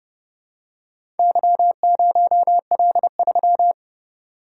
Morse code, heard as a keyed tone, about 29 wpm, 700 Hz. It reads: Y0L3